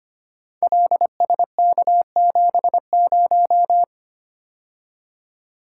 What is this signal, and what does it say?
Morse code, heard as a keyed tone, about 25 wpm, 700 Hz. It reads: LSK70
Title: Code Groups - 5 Letter:Numbers - 25wpm